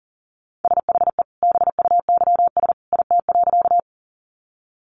Morse code, heard as a keyed tone, about 40 wpm, 700 Hz. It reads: SHE BUYS IT.
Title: Additional Easy Phrases 40wpm